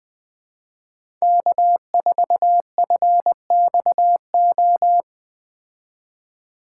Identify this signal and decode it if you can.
Morse code, heard as a keyed tone, about 21 wpm, 700 Hz. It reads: K4FXO